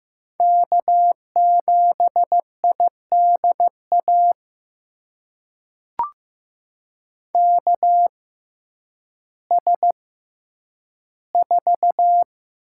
Morse code, heard as a keyed tone, about 15 wpm, spaced as for 9 wpm, 700 Hz. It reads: K7IDA KS4
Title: USA Call Signs - 5 Characters - 6x Character Spacing - 15wpm